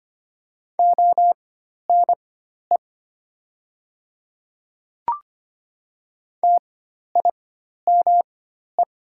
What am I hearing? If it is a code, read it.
Morse code, heard as a keyed tone, about 25 wpm, spaced as for 12 wpm, 690 Hz. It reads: ONE TIME
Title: 4x Character Spacing - Top 100 Words - 25wpm